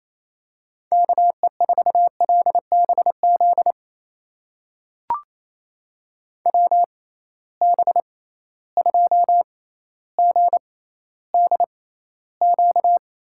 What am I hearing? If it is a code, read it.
Morse code, heard as a keyed tone, about 28 wpm, spaced as for 14 wpm, 700 Hz. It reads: KE4LBZ WB2GDQ
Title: USA Call Signs - 6 Characters - 6x Character Spacing - 28wpm